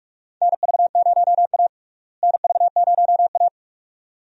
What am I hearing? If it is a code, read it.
Morse code, heard as a keyed tone, about 46 wpm, 700 Hz. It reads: NV0A NV0A
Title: USA Call Signs - 4 Characters - Repeated 2x - 45wpm